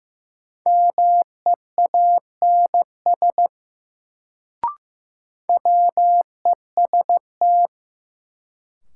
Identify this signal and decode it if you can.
Morse code, heard as a keyed tone, about 15 wpm, 700 Hz. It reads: MEANS WEST
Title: Sets of 5 Words - New Word Drill - 15wpm